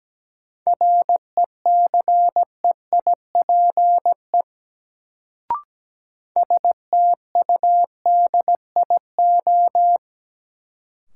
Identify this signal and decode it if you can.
Morse code, heard as a keyed tone, about 17 wpm, 700 Hz. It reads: RECEIPE STUDIO
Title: Sentences from Top 1200 Words - New Word Drill - 17wpm